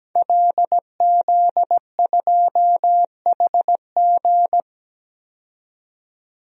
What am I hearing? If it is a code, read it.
Morse code, heard as a keyed tone, about 17 wpm, 700 Hz. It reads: LZ2HG
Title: DXCC-100 Call Signs - 5 Characters - 17wpm